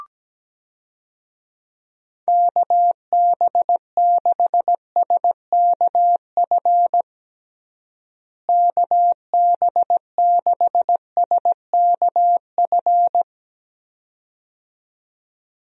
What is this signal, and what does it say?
Morse code, heard as a keyed tone, about 17 wpm, 700 Hz. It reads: KB6SKF KB6SKF
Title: USA Call Signs - 6 Characters - Repeated 2x - 17wpm